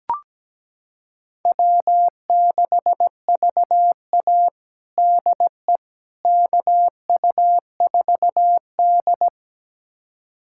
Morse code, heard as a keyed tone, about 17 wpm, 700 Hz. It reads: W6VA DE KU4D
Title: USA Callsign DE Callsign - 4 Characters - 17wpm